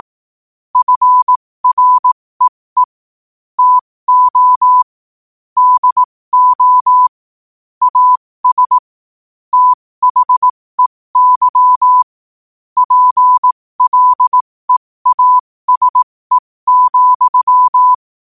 Morse code, heard as a keyed tone, about 18 wpm, spaced as for 15 wpm, 1000 Hz. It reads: FREE TO DO AS THEY PLEASE,
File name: qotdfiles / 15wpm